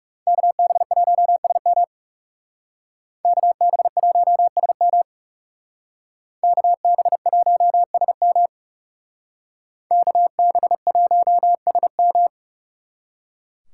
Morse code, heard as a keyed tone, about 39 wpm, 700 Hz. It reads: KB1SM KB1SM KB1SM KB1SM